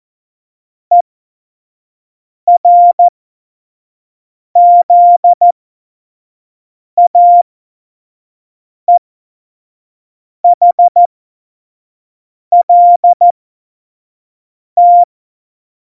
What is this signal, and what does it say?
Morse code, heard as a keyed tone, about 14 wpm, spaced as for 5 wpm, 700 Hz. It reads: ERZAEHLT